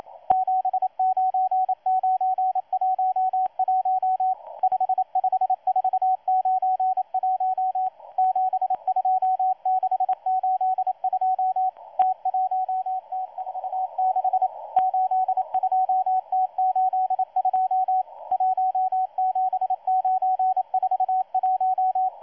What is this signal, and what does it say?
Morse code, heard as a keyed tone, about 27 wpm, 750 Hz, with radio noise. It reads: 79911 55491 72682 T1T46 82T82 17941
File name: M12-2016-07-07-1310utc-13926khz-error-restart